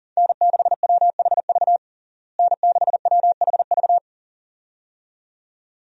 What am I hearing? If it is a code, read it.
Morse code, heard as a keyed tone, about 40 wpm, 700 Hz. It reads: N6WHV N6WHV
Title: USA Call Signs - 5 Characters - Repeated 2x - 40wpm